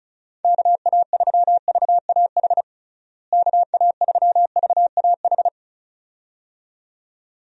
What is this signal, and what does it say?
Morse code, heard as a keyed tone, about 34 wpm, 700 Hz. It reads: KA3VAH KA3VAH
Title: USA Call Signs - 6 Characters - Repeated 2x - 35wpm